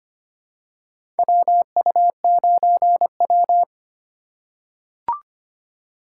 Morse code, heard as a keyed tone, about 25 wpm, 700 Hz. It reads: WU9W
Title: USA Call Signs - 4 Characters - 3x Character Spacing - 25wpm